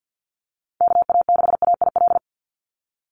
Morse code, heard as a keyed tone, about 50 wpm, 700 Hz. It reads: KA6AIL
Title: Call Signs - 50wpm